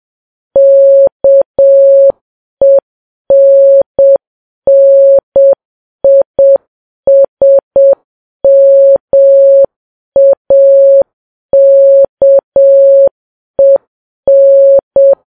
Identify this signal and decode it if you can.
Morse code, heard as a keyed tone, about 7 wpm, 550 Hz. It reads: KENNISMAKEN
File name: Files Web tls / Morse on9fv / morse7kps